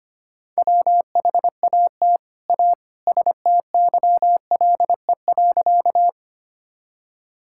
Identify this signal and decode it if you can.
Morse code, heard as a keyed tone, about 25 wpm, 700 Hz. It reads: WHAT A STYLE.
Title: Sentences from Top 1000 Words - Review Part 5 - 25wpm